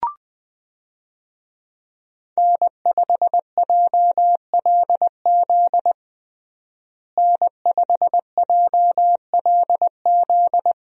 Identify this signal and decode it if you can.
Morse code, heard as a keyed tone, about 20 wpm, 700 Hz. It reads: N5JLZ N5JLZ
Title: USA Call Signs - 5 Characters - Repeated 2x - 20wpm